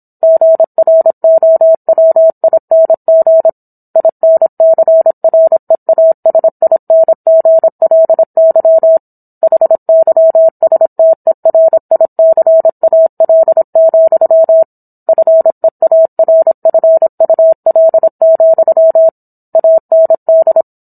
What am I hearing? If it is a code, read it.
Morse code, heard as a keyed tone, about 27 wpm, 650 Hz. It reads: GROWING INCREASINGLY HYSTERICAL, FEARFUL, AND